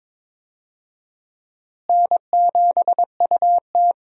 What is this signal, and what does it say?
Morse code, heard as a keyed tone, about 21 wpm, 700 Hz. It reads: N7UT
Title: USA Call Signs - 4 Characters - Repeated 2x - 22wpm